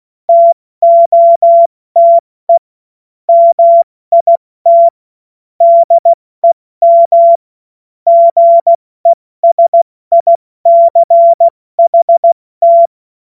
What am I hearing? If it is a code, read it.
Morse code, heard as a keyed tone, about 16 wpm, 680 Hz. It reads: TOTE MIT DEM GESICHT